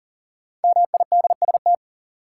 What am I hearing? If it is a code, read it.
Morse code, heard as a keyed tone, about 39 wpm, 700 Hz. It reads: MIDST
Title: Sets of 4 Words - New Word Drill - 40wpm